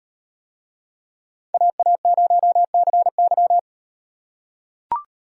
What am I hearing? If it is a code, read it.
Morse code, heard as a keyed tone, about 38 wpm, 700 Hz. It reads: AA0CY